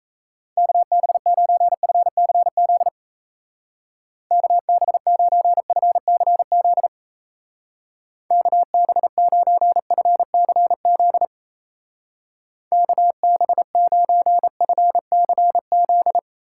Morse code, heard as a keyed tone, about 35 wpm, 700 Hz. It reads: KB9FCZ KB9FCZ KB9FCZ KB9FCZ
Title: Speed Racing - USA Call Signs 28wpm